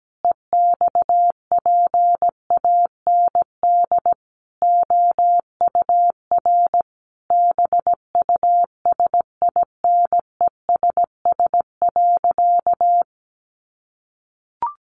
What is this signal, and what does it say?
Morse code, heard as a keyed tone, about 17 wpm, 700 Hz. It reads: EXPAND OUR BUSINESS.
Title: Sentences from Top 1400 Words 17wpm